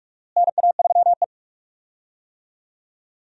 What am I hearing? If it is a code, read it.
Morse code, heard as a keyed tone, about 46 wpm, 700 Hz. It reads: NA3E
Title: USA Call Signs - 4 Characters - Repeated 3x - 45wpm